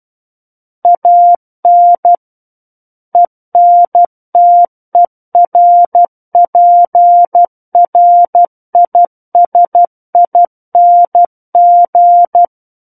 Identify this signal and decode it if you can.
Morse code, heard as a keyed tone, about 12 wpm, 700 Hz. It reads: AN ENTERPRISING
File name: Book One / 03-Chapter 3 - 12 WPM